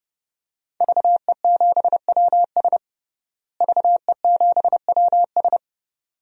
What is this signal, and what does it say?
Morse code, heard as a keyed tone, about 32 wpm, 700 Hz. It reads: VE7WS VE7WS
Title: DXCC-100 Call Signs - 5 Characters - Repeated 2x - 30wpm